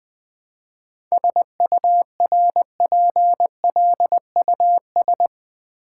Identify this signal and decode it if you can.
Morse code, heard as a keyed tone, about 20 wpm, 700 Hz. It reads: SURPLUS
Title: 7 Letter Words - First 3 Letters Given - 20wpm